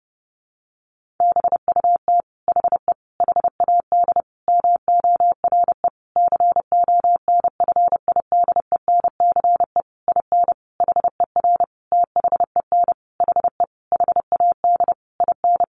Morse code, heard as a keyed tone, about 30 wpm, 700 Hz. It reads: BUT HE HAD MORE CONFIDENCE IN HER THEN HE HAD IN
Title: Sentences from Top 1500 Words 30wpm